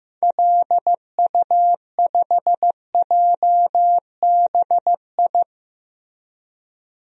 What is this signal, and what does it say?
Morse code, heard as a keyed tone, about 15 wpm, 700 Hz. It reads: LU5JBI
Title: DXCC-100 Call Signs - 6 Characters - Repeated 3x - 15wpm